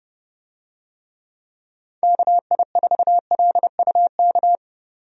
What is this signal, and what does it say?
Morse code, heard as a keyed tone, about 30 wpm, 700 Hz. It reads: KI4LUK
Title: USA Call Signs - 6 Characters - 30wpm